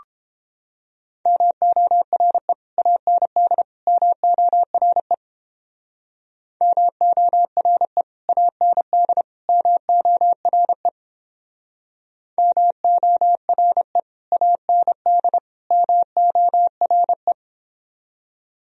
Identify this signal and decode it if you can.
Morse code, heard as a keyed tone, about 29 wpm, 700 Hz. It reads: MORE AND MORE MORE AND MORE MORE AND MORE